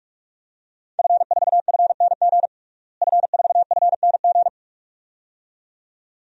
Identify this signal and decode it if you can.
Morse code, heard as a keyed tone, about 45 wpm, 700 Hz. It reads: F4FNG F4FNG
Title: DXCC-100 Call Signs - 5 Characters - Repeated 2x - 45wpm